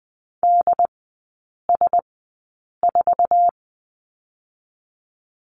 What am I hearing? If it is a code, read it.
Morse code, heard as a keyed tone, about 20 wpm, spaced as for 8 wpm, 700 Hz. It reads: DS4
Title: 074 Lesson 13e - Letter L - Three Characters 20wpm